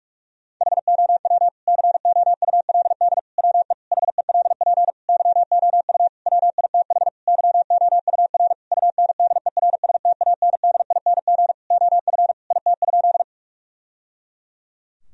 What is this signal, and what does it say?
Morse code, heard as a keyed tone, about 46 wpm, 700 Hz. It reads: HOW COULD WE HELP YOU WITH YOUR UNDERSTANDING OF IT?
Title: Sentences from Top 1000 Words - Review Part 3 - 45wpm